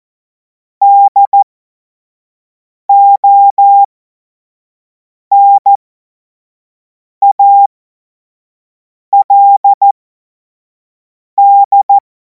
Morse code, heard as a keyed tone, about 14 wpm, spaced as for 5 wpm, 800 Hz. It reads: DONALD
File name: Morsekurs / Witze / Witz 26. Mai 2026 25 F70